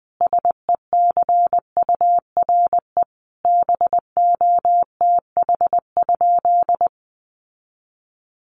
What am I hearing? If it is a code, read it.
Morse code, heard as a keyed tone, about 20 wpm, 700 Hz. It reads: SECURE BOTH?
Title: Sentences from Top 1200 Words 20wpm